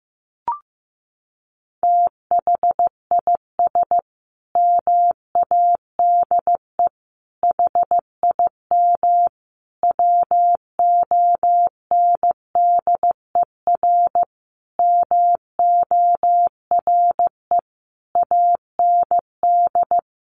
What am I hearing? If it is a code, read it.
Morse code, heard as a keyed tone, about 15 wpm, 700 Hz. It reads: THIS MADE HIM WONDER MORE AND
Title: Sentences from Top 500 Words 15wpm